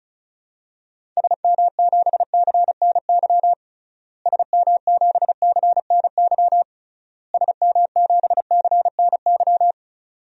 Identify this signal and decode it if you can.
Morse code, heard as a keyed tone, about 36 wpm, 700 Hz. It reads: SM7CNY SM7CNY SM7CNY
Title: DXCC-100 Call Signs - 6 Characters - Repeated 3x - 35wpm